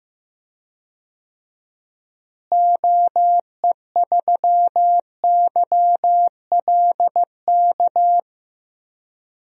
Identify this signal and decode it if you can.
Morse code, heard as a keyed tone, about 15 wpm, 700 Hz. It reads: OE3YLK
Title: DXCC-100 Call Signs - 6 Characters - Repeated 3x - 15wpm